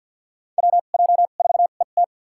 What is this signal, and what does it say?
Morse code, heard as a keyed tone, about 48 wpm, spaced as for 35 wpm, 700 Hz. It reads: WJ4ET
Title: USA Call Signs - 5 Characters - 2x Character Spacing - 50wpm